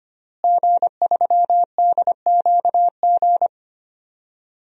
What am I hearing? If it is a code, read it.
Morse code, heard as a keyed tone, about 25 wpm, 700 Hz. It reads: G3DQG
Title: DXCC-100 Call Signs - 5 Characters - Repeated 3x - 25wpm